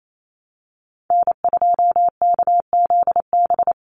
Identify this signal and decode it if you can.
Morse code, heard as a keyed tone, about 28 wpm, 700 Hz. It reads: N2KZB
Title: USA Call Signs 28wpm